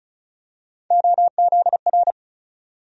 Morse code, heard as a keyed tone, about 36 wpm, 700 Hz. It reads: OZR
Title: USA Call Sign Postfixes - 3 Letters 35wpm